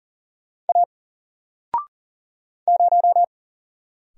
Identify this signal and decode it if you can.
Morse code, heard as a keyed tone, about 40 wpm, 700 Hz. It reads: A 0
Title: Single Letter-Number Alt 40wpm